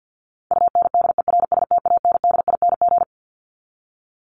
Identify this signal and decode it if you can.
Morse code, heard as a keyed tone, about 49 wpm, 700 Hz. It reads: UNDERSTANDING
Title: ING Suffix - Top 100 Words with Base Words 50wpm